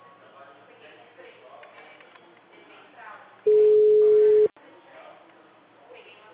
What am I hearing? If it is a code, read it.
Ringback tone (Europe, 425 Hz), with speech.
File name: out- -2013- - - .1536